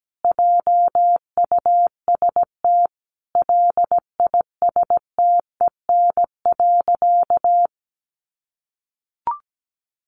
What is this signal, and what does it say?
Morse code, heard as a keyed tone, about 17 wpm, 700 Hz. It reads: JUST LISTEN.
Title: Sentences from Top 1300 Words 17wpm